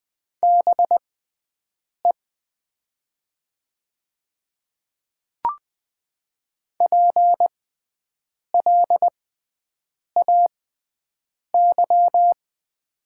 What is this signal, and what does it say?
Morse code, heard as a keyed tone, about 20 wpm, spaced as for 7 wpm, 700 Hz. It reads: BE PLAY
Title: 6x Character Spacing - Top 300 Words - 20wpm